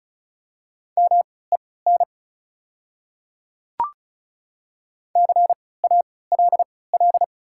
Morse code, heard as a keyed tone, about 36 wpm, spaced as for 20 wpm, 700 Hz. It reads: MEN CALL
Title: 3x Character Spacing - Top 300 Words - 35wpm